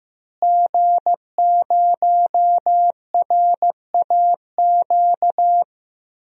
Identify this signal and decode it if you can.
Morse code, heard as a keyed tone, about 15 wpm, 700 Hz. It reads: G0RAQ
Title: DXCC-100 Call Signs - 5 Characters - Repeated 2x - 15wpm